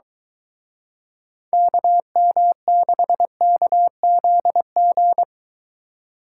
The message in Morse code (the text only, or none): KM6KZG